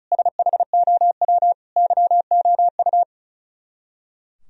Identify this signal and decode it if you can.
Morse code, heard as a keyed tone, about 45 wpm, 700 Hz. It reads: SHOW YOU